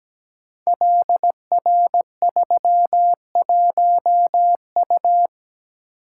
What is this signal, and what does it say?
Morse code, heard as a keyed tone, about 17 wpm, 700 Hz. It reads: LR31U
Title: Code Groups - 5 Letter:Numbers - 17wpm